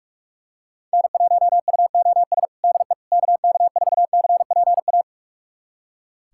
Morse code, heard as a keyed tone, about 46 wpm, 700 Hz. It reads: N1UOS DE KK4CPA
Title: USA Callsign DE Callsign 45wpm